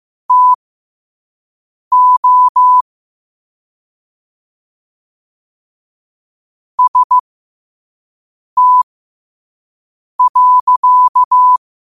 Morse code, heard as a keyed tone, about 15 wpm, spaced as for 5 wpm, 1000 Hz. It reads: TO ST.
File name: 5wpm